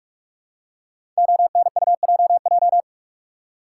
Morse code, heard as a keyed tone, about 44 wpm, 700 Hz. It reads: ONUJJ
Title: Code Groups - 5 Letters - 45wpm